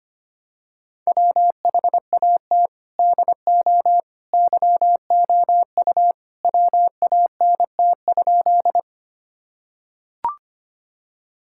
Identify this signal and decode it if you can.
Morse code, heard as a keyed tone, about 25 wpm, 700 Hz. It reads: WHAT DO YOU WANT?
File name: practice / 25wpm / Sentences with 2-Word Phrases - 25wpm